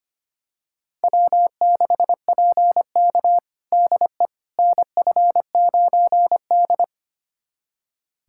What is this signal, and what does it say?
Morse code, heard as a keyed tone, about 25 wpm, 700 Hz. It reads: W6PK DE NF9D
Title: USA Callsign DE Callsign - 4 Characters - 25wpm